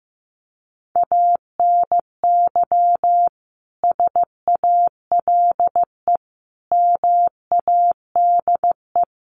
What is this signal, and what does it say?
Morse code, heard as a keyed tone, about 15 wpm, 700 Hz. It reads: ANY SALE MADE
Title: Sentences from Top 1400 Words 15wpm